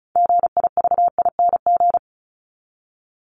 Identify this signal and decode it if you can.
Morse code, heard as a keyed tone, about 34 wpm, 700 Hz. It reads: GIVING
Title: ING Suffix - Top 100 Words with Base Words 35wpm